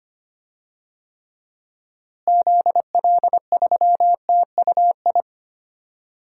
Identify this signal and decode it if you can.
Morse code, heard as a keyed tone, about 25 wpm, 700 Hz. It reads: ZL3TUI
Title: DXCC-100 Call Signs - 6 Characters - Repeated 2x - 25wpm